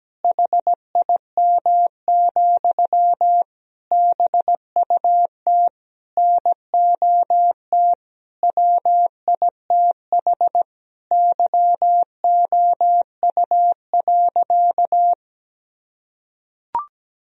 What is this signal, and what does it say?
Morse code, heard as a keyed tone, about 17 wpm, 700 Hz. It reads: HIM, BUT NOT WITH YOU.
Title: Sentences from Top 1000 Words - Review Part 4 - 17wpm